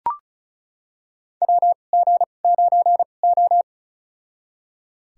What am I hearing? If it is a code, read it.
Morse code, heard as a keyed tone, about 34 wpm, spaced as for 25 wpm, 700 Hz. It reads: WG9O
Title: USA Call Signs - 4 Characters - 2x Character Spacing - 35wpm